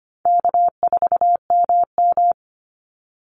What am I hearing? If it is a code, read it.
Morse code, heard as a keyed tone, about 25 wpm, 700 Hz. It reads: K4MM
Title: Call Signs - 25wpm